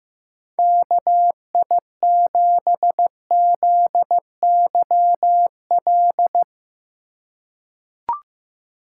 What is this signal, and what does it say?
Morse code, heard as a keyed tone, about 15 wpm, 700 Hz. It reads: KI7ZYL
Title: USA Call Signs - 4x Character Spacing - 15wpm